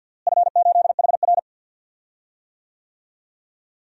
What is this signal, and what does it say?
Morse code, heard as a keyed tone, about 52 wpm, 700 Hz. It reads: F8HR
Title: DXCC-100 Call Signs - 4 Characters - Repeated 3x - 50wpm